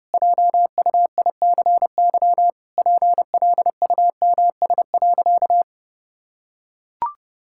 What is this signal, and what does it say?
Morse code, heard as a keyed tone, about 30 wpm, 700 Hz. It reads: JUICY PLUMS.
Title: Pangram Challenge v2 - 30wpm